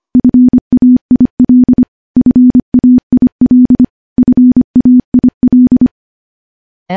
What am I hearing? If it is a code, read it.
Morse code, heard as a keyed tone, about 25 wpm, 250 Hz. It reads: FAIL FAIL FAIL
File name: morse / morse-training-25WPM